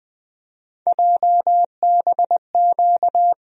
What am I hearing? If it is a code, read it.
Morse code, heard as a keyed tone, about 20 wpm, 700 Hz. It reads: JBQ
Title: USA Call Sign Postfixes - 3 Letters 20wpm